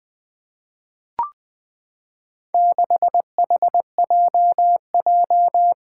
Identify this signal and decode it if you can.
Morse code, heard as a keyed tone, about 20 wpm, 700 Hz. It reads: 6HJJ
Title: Four Challenging Letter-Number Combinations 20wpm